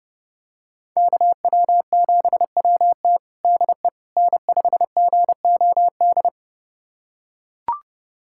Morse code, heard as a keyed tone, about 32 wpm, 700 Hz. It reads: KW7WT DE N5GOD
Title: USA Callsign DE Callsign - 5 Characters - 30wpm